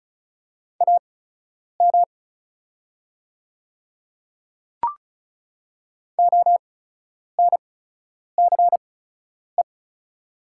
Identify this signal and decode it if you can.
Morse code, heard as a keyed tone, about 36 wpm, spaced as for 10 wpm, 700 Hz. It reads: AM ONCE
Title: 8x Character Spacing - Top 500 Words - 35wpm